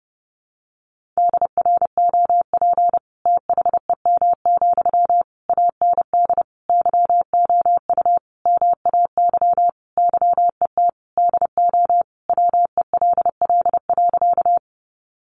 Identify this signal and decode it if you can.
Morse code, heard as a keyed tone, about 30 wpm, 700 Hz. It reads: DROP THEM, AND YOU MAY YET DO WELL.
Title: Sentences from Top 1400 Words 30wpm